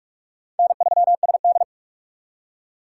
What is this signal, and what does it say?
Morse code, heard as a keyed tone, about 57 wpm, 700 Hz. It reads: N3SD